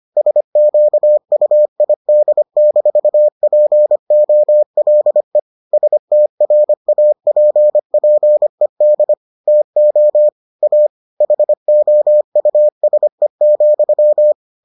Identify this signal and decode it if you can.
Morse code, heard as a keyed tone, about 25 wpm, 600 Hz. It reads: SQUID-POLE STRAPPED TO A HOUSE,